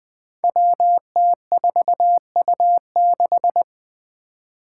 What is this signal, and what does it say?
Morse code, heard as a keyed tone, about 20 wpm, 700 Hz. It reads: WT4U6
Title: Five Challenging Letter-Number Combinations 20wpm